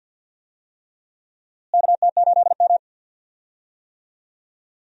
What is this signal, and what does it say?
Morse code, heard as a keyed tone, about 52 wpm, 700 Hz. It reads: KT8M
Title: USA Call Signs - 4 Characters - Repeated 3x - 50wpm